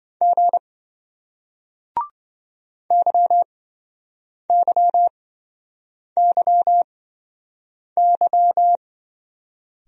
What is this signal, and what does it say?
Morse code, heard as a keyed tone, about 27 wpm, spaced as for 7 wpm, 700 Hz. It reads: G YYYY